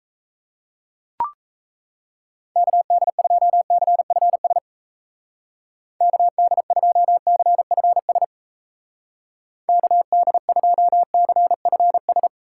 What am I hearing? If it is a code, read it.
Morse code, heard as a keyed tone, about 35 wpm, 700 Hz. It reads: KD2CFS KD2CFS KD2CFS
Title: Speed Racing - USA 2x3 Call Signs 28wpm